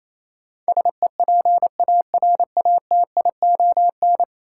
Morse code, heard as a keyed tone, about 28 wpm, 700 Hz. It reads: SEPARATION
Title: Sets of 5 Words - New Word Drill - 28wpm